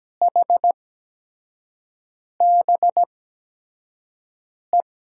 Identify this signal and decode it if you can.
Morse code, heard as a keyed tone, about 17 wpm, spaced as for 5 wpm, 700 Hz. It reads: HBE
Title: USA Call Signs - 6 Characters - 8x Character Spacing - 17wpm